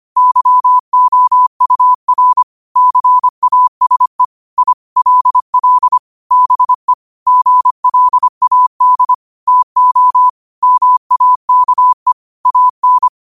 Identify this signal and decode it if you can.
Morse code, heard as a keyed tone, about 25 wpm, 1000 Hz. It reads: YOUR CASE I LL BE GLAD TO MAKE AN